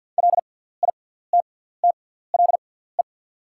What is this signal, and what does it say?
Morse code, heard as a keyed tone, about 48 wpm, spaced as for 17 wpm, 690 Hz. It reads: LITTLE
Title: 6x Character Spacing - Top 200 Words - 50wpm